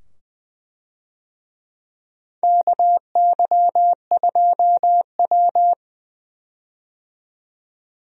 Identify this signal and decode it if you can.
Morse code, heard as a keyed tone, about 20 wpm, 700 Hz. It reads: KY2W
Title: USA Call Signs - 4 Characters - 20wpm